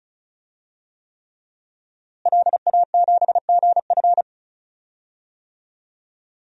Morse code, heard as a keyed tone, about 34 wpm, 700 Hz. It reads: LA7GF
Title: DXCC-100 Call Signs - 5 Characters - Repeated 3x - 35wpm